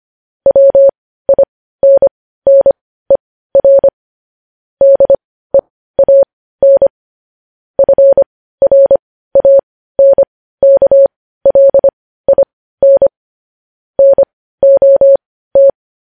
Morse code, heard as a keyed tone, about 25 wpm, spaced as for 15 wpm, 550 Hz. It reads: WINNER DEAN FRANKLIN NOT
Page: News headlines in morse code at 15 wpm